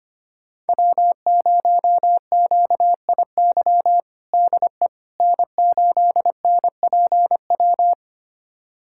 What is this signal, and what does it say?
Morse code, heard as a keyed tone, about 25 wpm, 700 Hz. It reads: W0QIY DE N8NPW
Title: USA Callsign DE Callsign - 5 Characters - 25wpm